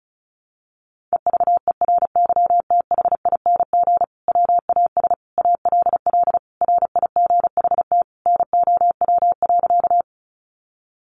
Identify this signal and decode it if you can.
Morse code, heard as a keyed tone, about 34 wpm, 700 Hz. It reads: EVERYTHING WAS ALL RIGHT NOW.
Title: Sentences from Top 700 Words 35wpm